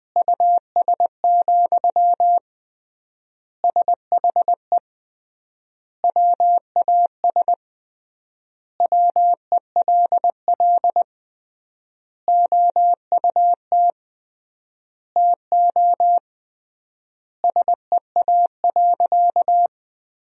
Morse code, heard as a keyed tone, about 20 wpm, 700 Hz. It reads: US, SHE WAS WELL OUT TO SEA.
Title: 3x Word-Spacing - Sentences from Top 400 Words - 20wpm